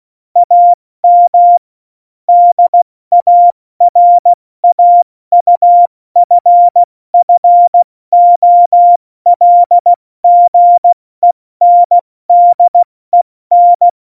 Morse code, heard as a keyed tone, about 16 wpm, 700 Hz. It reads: AM DARAUFFOLGENDEN
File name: Morsekurs / Witze / Witz 15. Oktober 2019, 70 F80